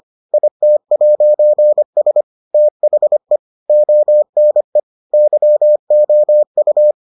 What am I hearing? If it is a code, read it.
Morse code, heard as a keyed tone, about 25 wpm, 600 Hz. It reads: IT'S THE ONE YOU